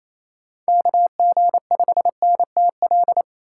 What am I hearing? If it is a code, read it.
Morse code, heard as a keyed tone, about 28 wpm, 700 Hz. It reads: KG5NTL
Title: USA Call Signs - 6 Characters - 4x Character Spacing - 28wpm